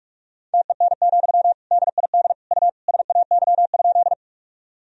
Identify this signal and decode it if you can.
Morse code, heard as a keyed tone, about 45 wpm, 700 Hz. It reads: TEN, DID U SAY?
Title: Sentences from Top 400 Words and Abbreviations 45wpm